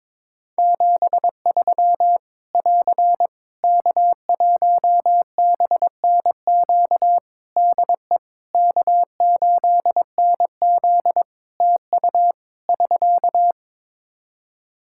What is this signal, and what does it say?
Morse code, heard as a keyed tone, about 22 wpm, 700 Hz. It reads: 73 <AR> K1BNQ DE K8NZ TU <SK>
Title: Practice QSO - Exchange 4 - Ending - 22wpm